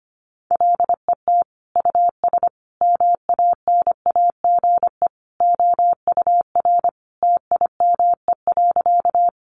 Morse code, heard as a keyed tone, about 25 wpm, 700 Hz. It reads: LET US MANAGE OUR TIME.
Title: Sentences from Top 1400 Words 25wpm